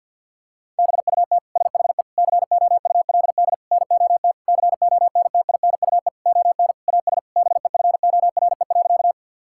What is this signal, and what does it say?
Morse code, heard as a keyed tone, about 50 wpm, 700 Hz. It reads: BUT SHE COULD NOT CONTINUE ON AS BEFORE.
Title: Sentences from Top 1000 Words - Review Part 6 - 50wpm